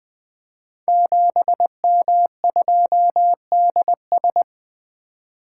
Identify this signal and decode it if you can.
Morse code, heard as a keyed tone, about 20 wpm, 700 Hz. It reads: 7M2DS
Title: Code Groups - 5 Letter:Numbers - 20wpm